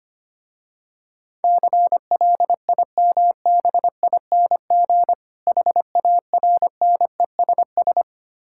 Morse code, heard as a keyed tone, about 25 wpm, 700 Hz. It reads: CLIMBING HARNESS
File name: practice / 25wpm / Difficult Words with Hints - 25wpm